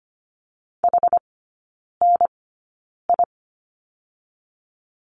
Morse code, heard as a keyed tone, about 25 wpm, spaced as for 9 wpm, 690 Hz. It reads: HNI
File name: Lesson 14e - Number 2 - Three Characters 25wpm